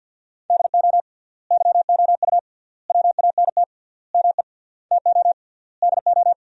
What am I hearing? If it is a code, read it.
Morse code, heard as a keyed tone, about 50 wpm, 700 Hz. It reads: DO YOU WANT ME TO DO
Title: 3x Word-Spacing - Sentences from Top 200 Words - 50wpm